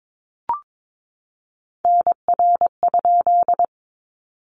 Morse code, heard as a keyed tone, about 22 wpm, 700 Hz. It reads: NR?
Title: Common QSO Phrases 22wpm